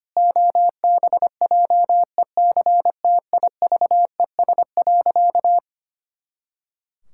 Morse code, heard as a keyed tone, about 25 wpm, 700 Hz. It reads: OBJECTIVES.
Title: Pangram Challenge v2 - 25wpm